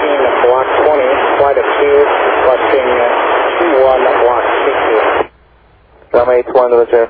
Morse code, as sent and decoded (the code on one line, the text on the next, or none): none